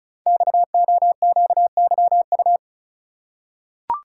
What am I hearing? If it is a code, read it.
Morse code, heard as a keyed tone, about 34 wpm, 700 Hz. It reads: XOQYU
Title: Code Groups - 5 Letters - 35wpm